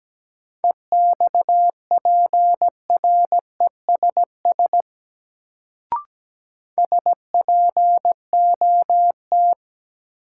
Morse code, heard as a keyed tone, about 17 wpm, 700 Hz. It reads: EXPRESS SPOT
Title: Sentences from Top 1100 Words - New Word Drill Alt - 17wpm